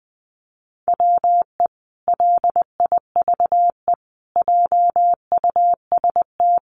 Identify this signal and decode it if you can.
Morse code, heard as a keyed tone, about 20 wpm, 700 Hz. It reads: WE LIVE JUST
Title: Sentences from Top 1300 Words 20wpm